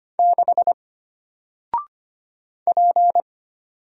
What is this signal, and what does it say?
Morse code, heard as a keyed tone, about 25 wpm, 700 Hz. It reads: 6 P
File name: Single-Letter-Number-25wpm